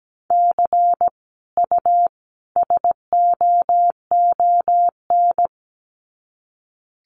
Morse code, heard as a keyed tone, about 17 wpm, 700 Hz. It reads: C U SOON
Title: Common QSO Phrases 17wpm